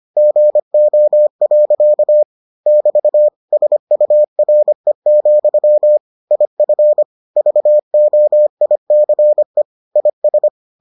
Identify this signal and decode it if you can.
Morse code, heard as a keyed tone, about 25 wpm, 600 Hz. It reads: GO. <BT> SURE, IF VOICE IS